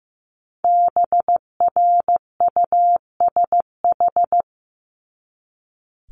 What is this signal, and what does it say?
Morse code, heard as a keyed tone, about 15 wpm, 700 Hz. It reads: BRUSH
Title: Top 1001-1250 Words 15wpm